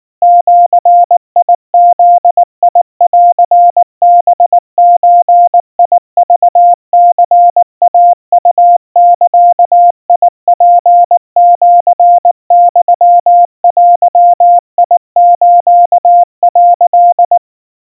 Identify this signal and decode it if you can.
Morse code (Japanese Wabun code), heard as a keyed tone, about 19 wpm, 700 Hz. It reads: ジブンハゾクニイウザツシユデス」
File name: kawa / Morse sample / kamyu 70 cpm 700 Hz Crispy